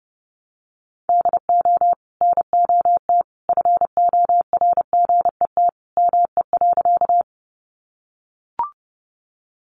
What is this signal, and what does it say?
Morse code, heard as a keyed tone, about 30 wpm, 700 Hz. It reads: DO NOT FORGET ME.
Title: Sentences from Top 1200 Words 30wpm